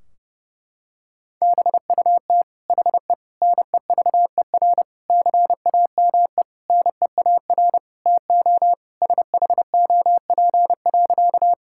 Morse code, heard as a keyed tone, about 32 wpm, 700 Hz. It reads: BUT HE NEVER CAME NEAR TO SHOP.
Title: Sentences from Top 1600 Words 30wpm